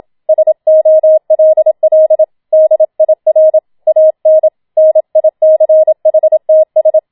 Morse code, heard as a keyed tone, about 27 wpm, 630 Hz. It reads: SOLL DIR AN NICHTS